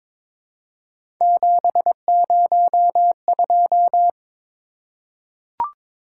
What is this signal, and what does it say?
Morse code, heard as a keyed tone, about 22 wpm, 700 Hz. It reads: 702
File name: practice / 22wpm / Three Letter-Number Combinations - 22wpm